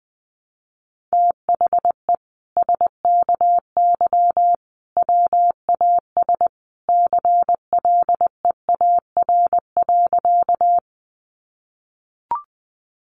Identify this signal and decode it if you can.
Morse code, heard as a keyed tone, about 20 wpm, 700 Hz. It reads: THE SKY WAS CLEAR.
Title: Sentences from Top 1200 Words 20wpm